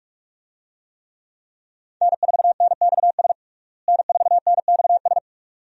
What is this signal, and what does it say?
Morse code, heard as a keyed tone, about 47 wpm, 700 Hz. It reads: N4NXS N4NXS
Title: USA Call Signs - 5 Characters - Repeated 2x - 45wpm